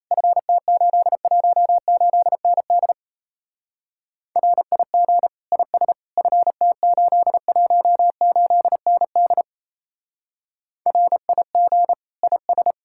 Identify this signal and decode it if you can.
Morse code, heard as a keyed tone, about 34 wpm, 700 Hz. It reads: FT818ND RIG IS FT818ND RIG IS
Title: Speed-Racing Common QSO Phrases - 28wpm